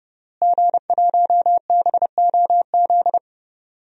Morse code, heard as a keyed tone, about 30 wpm, 700 Hz. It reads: G1BOZ
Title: DXCC-100 Call Signs - 5 Characters - Repeated 3x - 30wpm